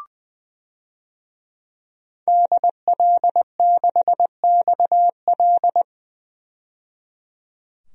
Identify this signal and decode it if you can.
Morse code, heard as a keyed tone, about 20 wpm, 700 Hz. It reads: DL6XL
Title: DXCC-100 Call Signs - 5 Characters - 20wpm